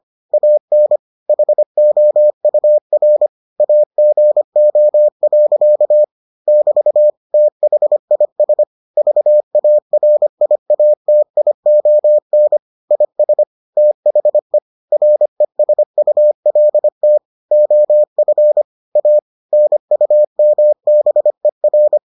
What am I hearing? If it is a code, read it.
Morse code, heard as a keyed tone, about 25 wpm, 600 Hz. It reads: AN HOUR AGO. <BT> THIS VARIATION IS THE RESULT OF A NUMBER